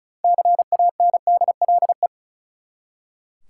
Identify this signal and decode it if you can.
Morse code, heard as a keyed tone, about 34 wpm, 700 Hz. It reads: CANDLE
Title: Six Letter Words - 35wpm